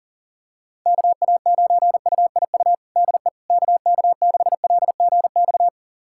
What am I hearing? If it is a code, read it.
Morse code, heard as a keyed tone, about 40 wpm, 700 Hz. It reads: KA9UIU DE KK6LGX
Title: USA Callsign DE Callsign - 6 Characters - 40wpm